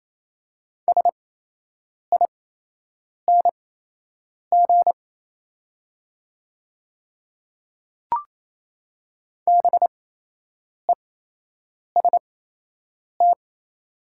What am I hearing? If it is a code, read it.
Morse code, heard as a keyed tone, about 28 wpm, spaced as for 8 wpm, 700 Hz. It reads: SING BEST
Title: 8x Character Spacing - Top 500 Words - 28wpm